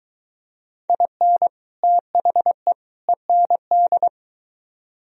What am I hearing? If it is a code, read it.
Morse code, heard as a keyed tone, about 23 wpm, 700 Hz. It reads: IN THE END